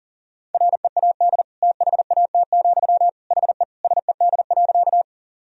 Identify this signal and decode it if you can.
Morse code, heard as a keyed tone, about 40 wpm, 700 Hz. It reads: READ THAT, HE SED.
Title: Sentences from Top 200 Words and Abbreviations 40wpm